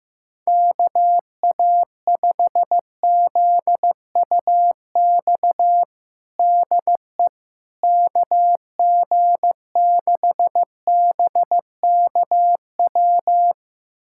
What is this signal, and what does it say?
Morse code, heard as a keyed tone, about 15 wpm, 700 Hz. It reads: KA5ZUX DE KG6BKW
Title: USA Callsign DE Callsign 15wpm